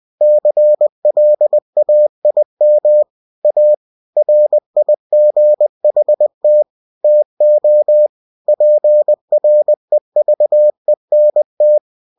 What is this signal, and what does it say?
Morse code, heard as a keyed tone, about 20 wpm, 600 Hz. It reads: CLAIM A RIGHT TO PREVENT
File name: cw-books / winnie-the-pooh / winnie-20wpm-page-0114